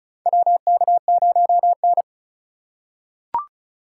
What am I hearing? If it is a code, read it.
Morse code, heard as a keyed tone, about 36 wpm, 700 Hz. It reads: WK0N
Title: USA Call Signs - 2x Character Spacing - 35wpm